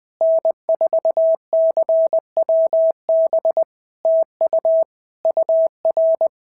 Morse code, heard as a keyed tone, about 20 wpm, 650 Hz. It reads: N4CWB TU UR
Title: POTA QSO Practice - 20wpm